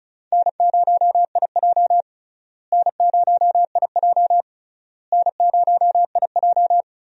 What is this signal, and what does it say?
Morse code, heard as a keyed tone, about 34 wpm, 700 Hz. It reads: N0IJ N0IJ N0IJ
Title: USA Call Signs - 4 Characters - Repeated 3x - 35wpm